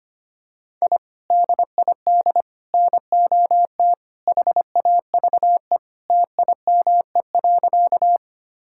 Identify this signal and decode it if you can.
Morse code, heard as a keyed tone, about 25 wpm, 700 Hz. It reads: I DID NOT HAVE TIME.
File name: practice / 25wpm / Sentences with 3-Word Phrases - 25wpm